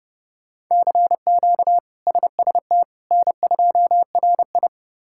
Morse code, heard as a keyed tone, about 30 wpm, 700 Hz. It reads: CQ SST N2RI
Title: K1USN SST QSO Practice - 30wpm